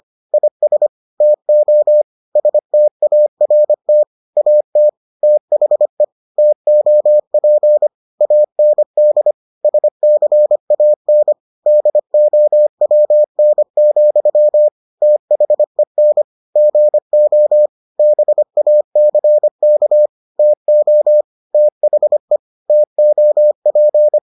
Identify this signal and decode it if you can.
Morse code, heard as a keyed tone, about 25 wpm, 600 Hz. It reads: IS TO START AT THE TOP AND SCAN DOWN, THEN GO BACK TO THE TOP